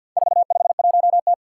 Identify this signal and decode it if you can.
Morse code, heard as a keyed tone, about 52 wpm, 700 Hz. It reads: 451T
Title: Four Challenging Letter-Number Combinations 50wpm